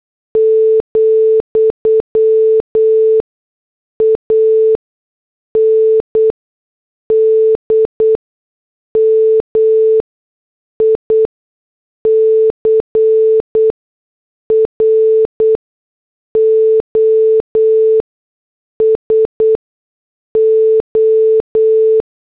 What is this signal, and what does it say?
Morse code, heard as a keyed tone, about 8 wpm, spaced as for 6 wpm, 440 Hz. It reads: , AND MICROSO
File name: morse / headlines